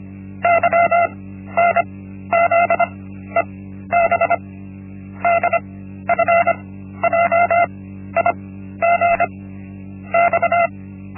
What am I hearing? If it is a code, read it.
Morse code, heard as a keyed tone, about 25 wpm, 680 Hz. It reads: Y N Z E B D F J I G X
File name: Linksmakalnis-1